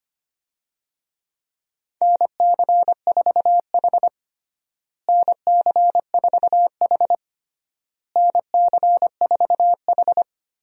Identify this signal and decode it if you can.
Morse code, heard as a keyed tone, about 25 wpm, 700 Hz. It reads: NC4H NC4H NC4H
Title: USA Call Signs - 4 Characters - Repeated 3x - 25wpm